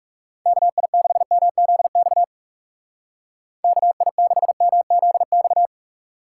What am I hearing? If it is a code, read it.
Morse code, heard as a keyed tone, about 37 wpm, 700 Hz. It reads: KI6MZX KI6MZX
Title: Speed Racing - USA 2x3 Call Signs 30wpm